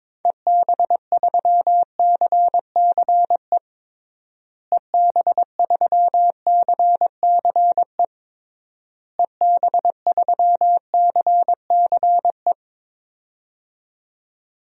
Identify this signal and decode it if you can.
Morse code, heard as a keyed tone, about 22 wpm, 700 Hz. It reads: EB3CCE EB3CCE EB3CCE
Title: DXCC-100 Call Signs - 6 Characters - Repeated 3x - 22wpm